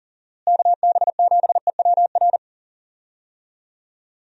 Morse code, heard as a keyed tone, about 38 wpm, 700 Hz. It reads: KB7EWR
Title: USA Call Signs - 6 Characters - Repeated 2x - 40wpm